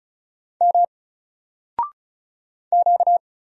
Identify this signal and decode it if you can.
Morse code, heard as a keyed tone, about 36 wpm, 700 Hz. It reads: M Q
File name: Single-Letter-Number-35wpm